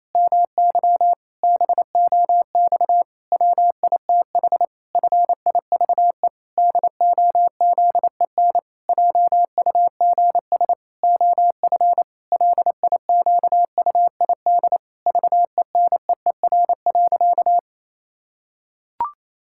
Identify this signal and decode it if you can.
Morse code, heard as a keyed tone, about 28 wpm, 700 Hz. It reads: MY BOX WITH FIVE DOZEN JUGS OF LIQUID VENEER.
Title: Pangram Challenge v2 - 28wpm